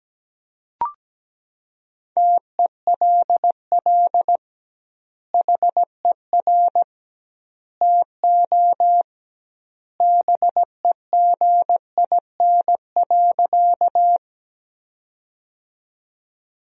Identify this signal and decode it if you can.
Morse code, heard as a keyed tone, about 17 wpm, 700 Hz. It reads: TELL HER TO BEGIN.
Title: 2x Word-Spacing - Sentences from Top 300 Words - 17wpm